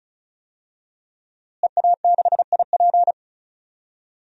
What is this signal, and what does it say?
Morse code, heard as a keyed tone, about 34 wpm, 700 Hz. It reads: EA6IP
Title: DXCC-100 Call Signs - 5 Characters - Repeated 3x - 35wpm